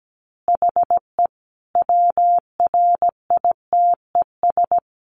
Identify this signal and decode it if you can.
Morse code, heard as a keyed tone, about 17 wpm, 700 Hz. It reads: HE WRITES
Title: Additional Easy Phrases 17wpm